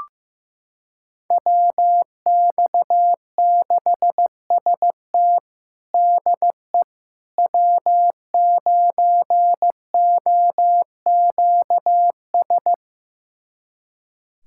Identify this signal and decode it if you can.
Morse code, heard as a keyed tone, about 15 wpm, 700 Hz. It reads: WX6ST DE W9OQS
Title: USA Callsign DE Callsign - 5 Characters - 15wpm